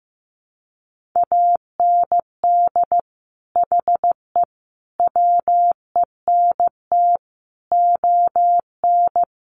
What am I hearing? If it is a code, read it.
Morse code, heard as a keyed tone, about 15 wpm, 700 Hz. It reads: AND HE WENT ON
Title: Sentences from Top 800 Words 15wpm